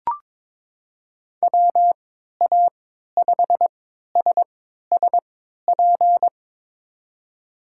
Morse code, heard as a keyed tone, about 22 wpm, spaced as for 12 wpm, 700 Hz. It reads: WA5SSP
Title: USA Call Signs - 6 Characters - 3x Character Spacing - 22wpm